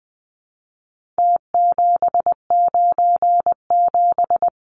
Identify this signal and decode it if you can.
Morse code, heard as a keyed tone, about 19 wpm, 700 Hz. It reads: T797
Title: Numbers - Contest Style - 4-Digit 20wpm